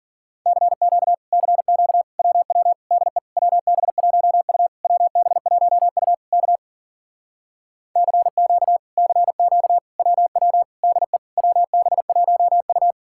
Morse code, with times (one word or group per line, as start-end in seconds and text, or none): 0.5-1.1 CQ
1.3-2.0 CQ
2.2-2.7 WW
2.9-3.2 DE
3.4-4.7 WB1U
4.8-6.1 WB1U
6.3-6.6 K
8.0-8.8 CQ
9.0-9.8 CQ
10.0-10.6 WW
10.8-11.2 DE
11.4-12.9 WB1U